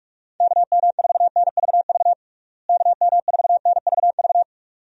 Morse code, heard as a keyed tone, about 46 wpm, 700 Hz. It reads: KM4NVV KM4NVV
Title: USA Call Signs - 6 Characters - Repeated 2x - 45wpm